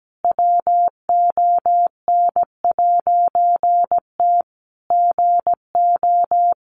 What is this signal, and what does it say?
Morse code, heard as a keyed tone, about 17 wpm, 700 Hz. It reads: WON'T GO
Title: A Sound Of Thunder 17wpm